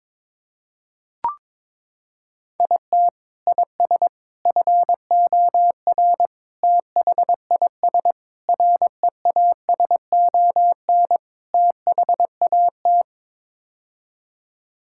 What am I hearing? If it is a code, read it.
Morse code, heard as a keyed tone, about 22 wpm, 700 Hz. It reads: IT IS FOR THIS REASON THAT
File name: practice / 22wpm / Sets of 6 Words - 22wpm